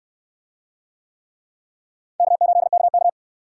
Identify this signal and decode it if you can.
Morse code, heard as a keyed tone, about 64 wpm, 700 Hz. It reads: K8KK